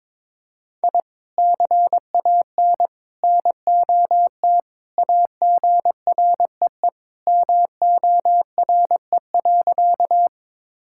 Morse code, heard as a keyed tone, about 22 wpm, 700 Hz. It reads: I CAN NOT AGREE MORE.
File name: practice / 22wpm / Sentences with 3-Word Phrases - 22wpm